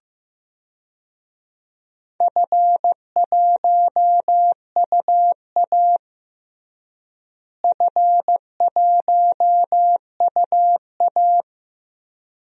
Morse code, heard as a keyed tone, about 15 wpm, 700 Hz. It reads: F1UA F1UA
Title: DXCC-100 Call Signs - 4 Characters - Repeated 2x - 15wpm